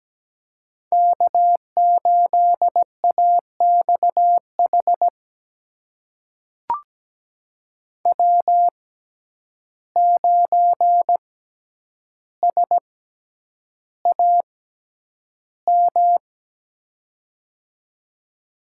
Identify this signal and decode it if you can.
Morse code, heard as a keyed tone, about 17 wpm, spaced as for 9 wpm, 700 Hz. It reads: K8AXH W9SAM
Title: USA Call Signs - 5 Characters - 6x Character Spacing - 17wpm